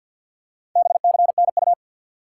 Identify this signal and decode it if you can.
Morse code, heard as a keyed tone, about 48 wpm, 700 Hz. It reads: BCNU
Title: Top 90 QSO Elements 50wpm